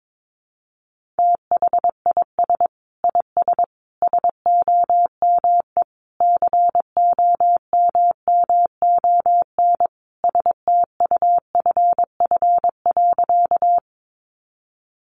Morse code, heard as a keyed tone, about 22 wpm, 700 Hz. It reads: THIS IS SOME COMMON STUFF.
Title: Sentences from Top 1400 Words 22wpm